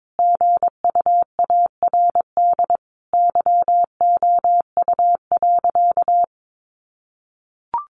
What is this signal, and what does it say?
Morse code, heard as a keyed tone, about 22 wpm, 700 Hz. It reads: GUARD YOU.
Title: Sentences from Top 700 Words 22wpm